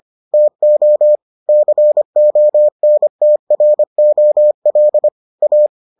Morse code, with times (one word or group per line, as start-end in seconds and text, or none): 0.3-1.2 TO
1.5-5.1 CONTROL
5.4-5.7 A